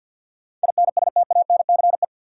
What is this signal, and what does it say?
Morse code, heard as a keyed tone, about 53 wpm, 700 Hz. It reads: INSTANCE
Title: Sentences from Top 800 Words - New Word Drill - 50wpm